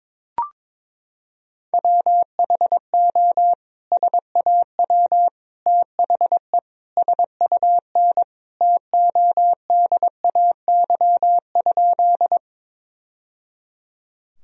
Morse code, heard as a keyed tone, about 22 wpm, 700 Hz. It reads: WHO SAW THE SUN TODAY?
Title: Sentences from Top 1000 Words - Review Part 1 - 22wpm